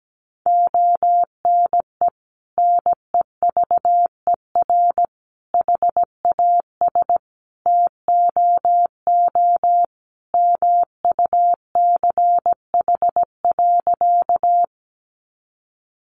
Morse code, heard as a keyed tone, about 17 wpm, 700 Hz. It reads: ONE NEVER HAS TOO MUCH.
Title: Sentences from Top 300 Words 17wpm